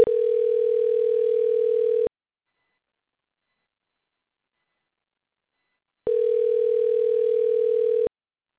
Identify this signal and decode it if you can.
Ringback tone (North America), 440 + 480 Hz.
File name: recordings / RE4f0bde00990b6f1726d2ff4f a7